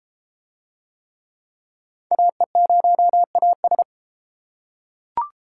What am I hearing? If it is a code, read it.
Morse code, heard as a keyed tone, about 33 wpm, 700 Hz. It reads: AE0AS